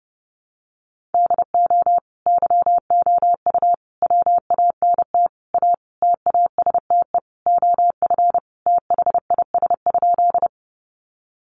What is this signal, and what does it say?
Morse code, heard as a keyed tone, about 30 wpm, 700 Hz. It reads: DO YOU WANT A TASTE OF THIS?
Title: Sentences from Top 1100 Words 30wpm